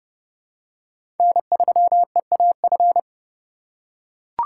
Morse code, heard as a keyed tone, about 32 wpm, 700 Hz. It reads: N3EAF
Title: USA Call Signs - 5 Characters - 4x Character Spacing - 30wpm